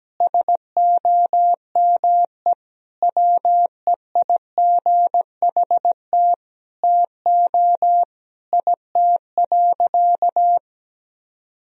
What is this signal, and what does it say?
Morse code, heard as a keyed tone, about 17 wpm, 700 Hz. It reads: SOME WEIGHT TO IT.
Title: Sentences from Top 1000 Words - Review Part 3 - 17wpm